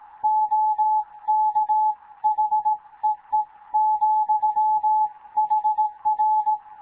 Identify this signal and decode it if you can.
Morse code, heard as a keyed tone, about 17 wpm, 860 Hz. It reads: OK HEE ,HR
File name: Morze-HA2MN / hangp / 2 11 GYAKORLAT